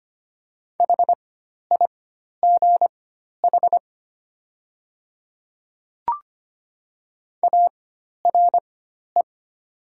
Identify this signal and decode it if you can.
Morse code, heard as a keyed tone, about 25 wpm, spaced as for 12 wpm, 700 Hz. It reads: HIGH ARE
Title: 4x Character Spacing - Top 200 Words - 25wpm